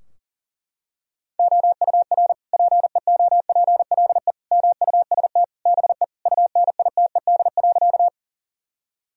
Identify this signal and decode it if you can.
Morse code, heard as a keyed tone, about 40 wpm, 700 Hz. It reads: OUR PEOPLE MUST BE UNITED.
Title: Sentences from Top 1000 Words - Review Part 4 - 40wpm